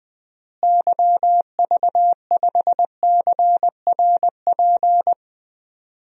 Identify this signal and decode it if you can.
Morse code, heard as a keyed tone, about 20 wpm, 700 Hz. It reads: YV5CRP
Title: DXCC-100 Call Signs - 6 Characters - Repeated 3x - 20wpm